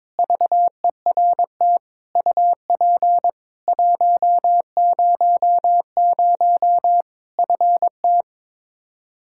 Morse code, heard as a keyed tone, about 22 wpm, 700 Hz. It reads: VERT UP 100 FT
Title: Practice QSO - Exchange 2 - Info - 22wpm